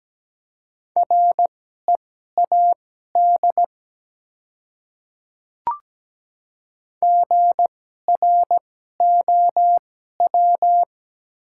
Morse code, heard as a keyed tone, about 17 wpm, spaced as for 12 wpm, 690 Hz. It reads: READ GROW
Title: 2x Character Spacing - Top 300 Words - 17wpm